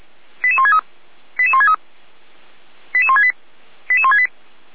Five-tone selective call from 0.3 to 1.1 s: 88163

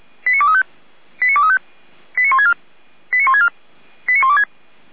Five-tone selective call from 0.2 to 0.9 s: 89235

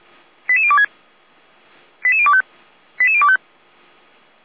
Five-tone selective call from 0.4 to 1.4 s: 88026

